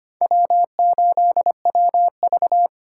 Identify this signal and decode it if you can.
Morse code, heard as a keyed tone, about 25 wpm, 700 Hz. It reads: W8WV
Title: USA Call Signs - 4 Characters - 2x Character Spacing - 25wpm